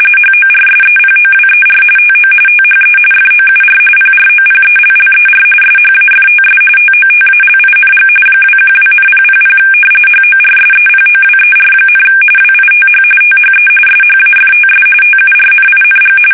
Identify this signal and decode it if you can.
RTTY, about 45.45 baud. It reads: HORSEPOWER TO THE FOUR PROPELLER SHAFTS TOTALING 212,000 HORSEPOWER, WITH THE SHAFTS TURNING
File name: fsk850 / battleshipnj